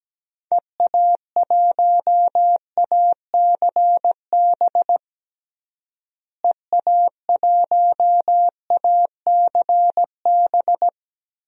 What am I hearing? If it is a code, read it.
Morse code, heard as a keyed tone, about 17 wpm, 700 Hz. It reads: EA1ACB EA1ACB
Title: DXCC-100 Call Signs - 6 Characters - Repeated 3x - 17wpm